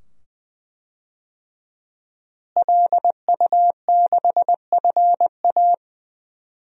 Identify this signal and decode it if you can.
Morse code, heard as a keyed tone, about 20 wpm, 700 Hz. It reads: LU6FA
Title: DXCC-100 Call Signs - 5 Characters - Repeated 2x - 20wpm